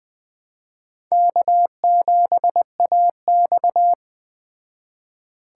Morse code, heard as a keyed tone, about 21 wpm, 700 Hz. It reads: K7AX
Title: Speed Racing - USA 1x2 Call Signs 17wpm